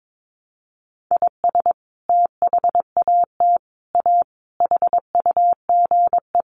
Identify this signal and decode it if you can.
Morse code, heard as a keyed tone, about 22 wpm, 700 Hz. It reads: IS THAT A HUGE
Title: Sentences from Top 1100 Words 22wpm